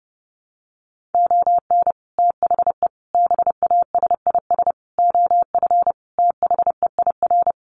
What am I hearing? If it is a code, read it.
Morse code, heard as a keyed tone, about 30 wpm, 700 Hz. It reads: ON THE BASIS OF THEIR
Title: Sets of 5 Words Encore 30wpm